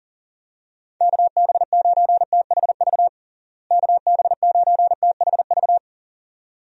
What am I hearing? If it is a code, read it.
Morse code, heard as a keyed tone, about 37 wpm, 700 Hz. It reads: KB9THV KB9THV
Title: USA Call Signs - 6 Characters - Repeated 2x - 40wpm